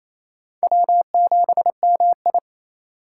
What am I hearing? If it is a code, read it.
Morse code, heard as a keyed tone, about 35 wpm, 700 Hz. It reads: W7MI